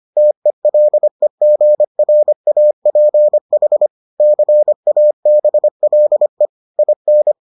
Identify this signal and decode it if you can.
Morse code, heard as a keyed tone, about 25 wpm, 600 Hz. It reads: TELEGRAPH CABLE IN